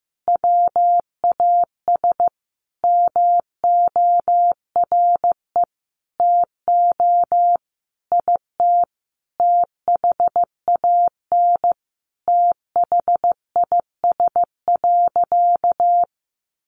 Morse code, heard as a keyed tone, about 15 wpm, 700 Hz. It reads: WAS MORE TO IT THAN THIS.
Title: Sentences from Top 100 Words 15wpm